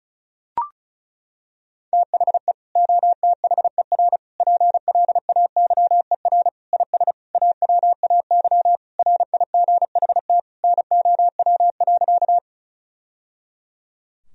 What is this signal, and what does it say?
Morse code, heard as a keyed tone, about 36 wpm, 700 Hz. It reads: THE OTHER PLAYER IS AWAY RIGHT NOW.
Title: Sentences from Top 1000 Words - Review Part 4 - 35wpm